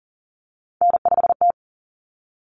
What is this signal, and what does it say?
Morse code, heard as a keyed tone, about 40 wpm, 700 Hz. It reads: N5T
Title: Common Cut Numbers - 3-Digit 40wpm